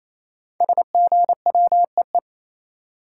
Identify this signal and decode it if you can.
Morse code, heard as a keyed tone, about 27 wpm, 700 Hz. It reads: SGWEE
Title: Code Groups - 5 Letters - 28wpm